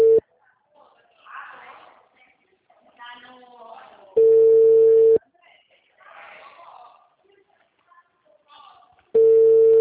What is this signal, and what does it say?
Ringback tone (China, 450 Hz), with speech.